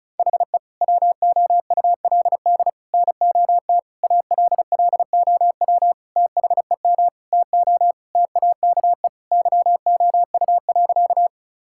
Morse code, heard as a keyed tone, about 34 wpm, 700 Hz. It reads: HE WOULD NOT ALLOW THEM TO TAKE YOU.
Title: Sentences from Top 1000 Words - Review Part 5 - 35wpm